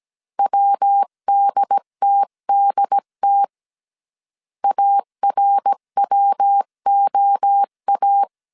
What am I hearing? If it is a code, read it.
Morse code, heard as a keyed tone, about 17 wpm, 800 Hz. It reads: WDTDT ARWOA
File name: CW / DET / 1 3 1500C S A 16wpm 2141